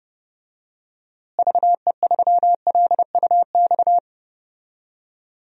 Morse code, heard as a keyed tone, about 30 wpm, 700 Hz. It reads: VE3LUX
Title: DXCC-100 Call Signs - 6 Characters - Repeated 3x - 30wpm